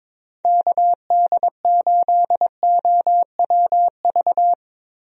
Morse code, heard as a keyed tone, about 22 wpm, 700 Hz. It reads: KD8OWV
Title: USA Call Signs - 6 Characters - Repeated 2x - 22wpm